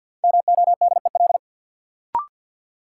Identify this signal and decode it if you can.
Morse code, heard as a keyed tone, about 49 wpm, 700 Hz. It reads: MODEL
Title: Sentences from Top 1000 Words - New Word Drill - 50wpm